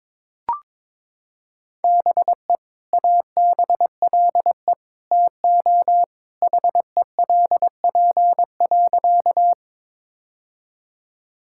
Morse code, heard as a keyed tone, about 22 wpm, 700 Hz. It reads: BE ABLE TO HELP.
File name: practice / 22wpm / Sentences with 2-Word Phrases - 22wpm